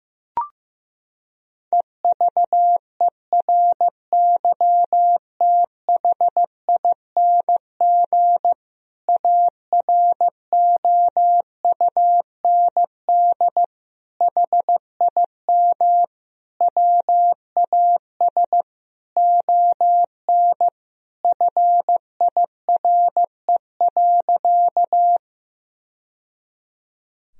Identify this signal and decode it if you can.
Morse code, heard as a keyed tone, about 16 wpm, 700 Hz. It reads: EVERYTHING AROUND HIM WAS ON FIRE.
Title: Sentences from Top 1000 Words - Review Part 3 - 15wpm